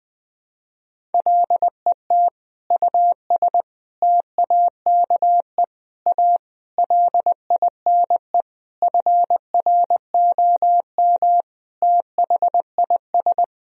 Morse code, heard as a keyed tone, about 20 wpm, 700 Hz. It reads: LET US TAKE A LINE FROM THIS
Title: Sentences from Top 1000 Words - Review Part 1 - 20wpm